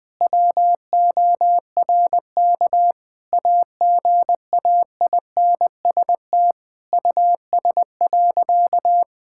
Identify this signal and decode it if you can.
Morse code, heard as a keyed tone, about 20 wpm, 700 Hz. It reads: WORK AGAINST US.
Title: Sentences from Top 1000 Words - Review Part 6 - 20wpm